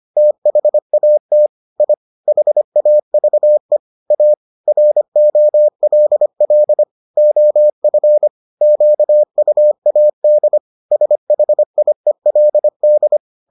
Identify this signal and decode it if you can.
Morse code, heard as a keyed tone, about 25 wpm, 600 Hz. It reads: THAT I HAVE A ROLL OF QUAD SHIELD